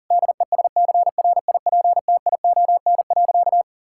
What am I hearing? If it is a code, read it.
Morse code, heard as a keyed tone, about 37 wpm, 700 Hz. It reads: DESCRIPTION.
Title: Sentences from Top 1600 Words 40wpm